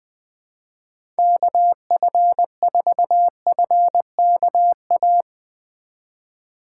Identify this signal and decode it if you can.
Morse code, heard as a keyed tone, about 20 wpm, 700 Hz. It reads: KF4FKA
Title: USA Call Signs - 6 Characters - 3x Character Spacing - 20wpm